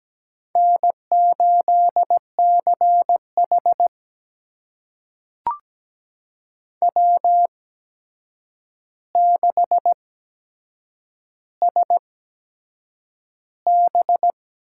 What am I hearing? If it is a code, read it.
Morse code, heard as a keyed tone, about 17 wpm, spaced as for 7 wpm, 700 Hz. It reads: N8CH W6SB
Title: USA Call Signs - 4 Characters - 8x Character Spacing - 17wpm